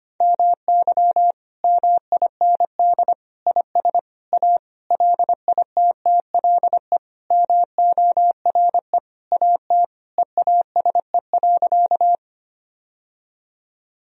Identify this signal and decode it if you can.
Morse code, heard as a keyed tone, about 25 wpm, 700 Hz. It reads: MY MIND IS A LITTLE MORE AT EASE.
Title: Sentences from Top 1000 Words - Review Part 1 - 25wpm